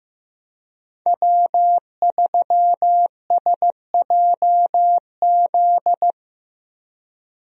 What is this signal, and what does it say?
Morse code, heard as a keyed tone, about 15 wpm, 700 Hz. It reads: W3SJZ
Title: USA Call Signs - 5 Characters - Repeated 2x - 15wpm